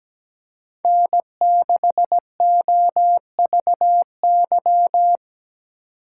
Morse code, heard as a keyed tone, about 17 wpm, 700 Hz. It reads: N6OVY
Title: USA Call Signs - 5 Characters - 2x Character Spacing - 17wpm